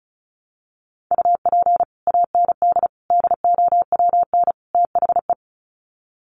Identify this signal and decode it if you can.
Morse code, heard as a keyed tone, about 34 wpm, 700 Hz. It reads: UP AND DOWN THE
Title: Sets of 4 Words 35wpm